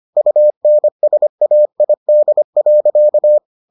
Morse code, heard as a keyed tone, about 25 wpm, 600 Hz. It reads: UNSAID.